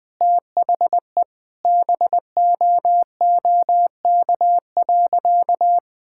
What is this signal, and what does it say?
Morse code, heard as a keyed tone, about 20 wpm, 700 Hz. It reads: THE BOOK.
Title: Sentences with 2-Word Phrases - 20wpm